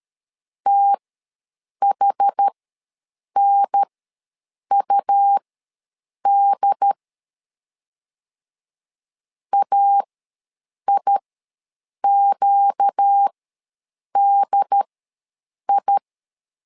Morse code, heard as a keyed tone, about 13 wpm, spaced as for 7 wpm, 800 Hz. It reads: THNUD AIQDI
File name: LEZIONI / 4 Q ws 12wpm 1912